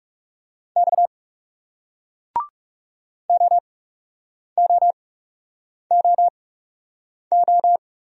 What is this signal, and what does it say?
Morse code, heard as a keyed tone, about 38 wpm, 700 Hz. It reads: X O O O O